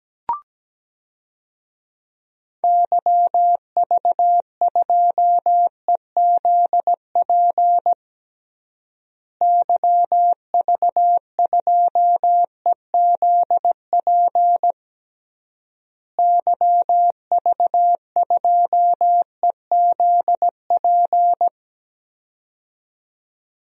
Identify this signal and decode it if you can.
Morse code, heard as a keyed tone, about 17 wpm, 700 Hz. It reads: YV2EZP YV2EZP YV2EZP
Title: DXCC-100 Call Signs - 6 Characters - Repeated 3x - 17wpm